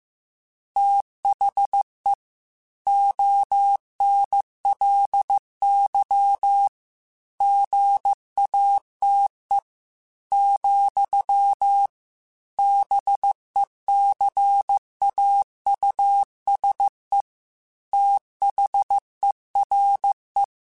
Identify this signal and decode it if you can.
Morse code, heard as a keyed tone, about 15 wpm, 800 Hz. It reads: THE ONLY GATE, BECAUSE THERE
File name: Technical Subjects / Technical Subjects-mp3s / AND Gate @13